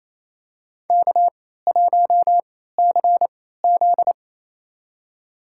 Morse code, heard as a keyed tone, about 28 wpm, spaced as for 16 wpm, 700 Hz. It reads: K1CZ
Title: USA Call Signs - 3x Character Spacing - 28wpm